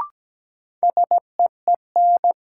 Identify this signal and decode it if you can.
Morse code, heard as a keyed tone, about 17 wpm, 700 Hz. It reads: SEEN
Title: Sets of 5 Words Encore - New Word Drill - 17wpm